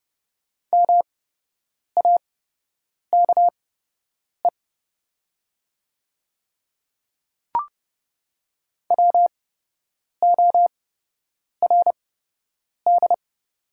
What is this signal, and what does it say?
Morse code, heard as a keyed tone, about 30 wpm, spaced as for 8 wpm, 700 Hz. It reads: MAKE WORD
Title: 8x Character Spacing - Top 100 Words - 30wpm